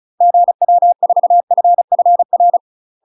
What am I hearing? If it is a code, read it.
Morse code, heard as a keyed tone, about 36 wpm, 700 Hz. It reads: GW4FFR
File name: UK MASTER QSOs TRAINING / 35WPM / UK MASTER QSOs 0020